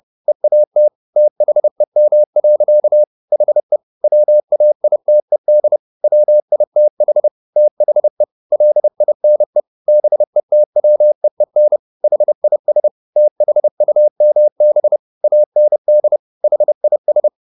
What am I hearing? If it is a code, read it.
Morse code, heard as a keyed tone, about 30 wpm, 600 Hz. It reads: EAT THEM. HE WAITED WITH THE LINE BETWEEN HIS THUMB AND HIS
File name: the old man and the sea / the old man and the sea 1 30wpm